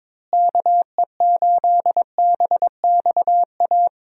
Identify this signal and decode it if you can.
Morse code, heard as a keyed tone, about 21 wpm, 700 Hz. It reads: KE8BXA
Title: USA Call Signs - 6 Characters - Repeated 3x - 22wpm